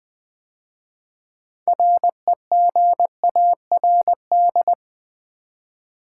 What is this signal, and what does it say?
Morse code, heard as a keyed tone, about 20 wpm, 700 Hz. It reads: REGARD
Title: 6 Letter Words - First 3 Letters Given - 20wpm